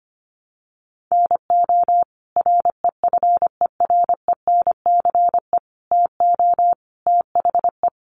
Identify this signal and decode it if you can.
Morse code, heard as a keyed tone, about 25 wpm, 700 Hz. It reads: NO REFERENCE TO THE
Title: Sentences from Top 1200 Words 25wpm